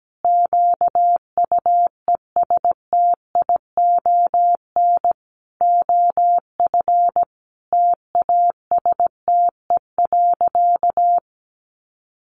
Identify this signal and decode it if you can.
Morse code, heard as a keyed tone, about 17 wpm, 700 Hz. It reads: QUESTION OF TASTE.
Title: Sentences from Top 1100 Words 17wpm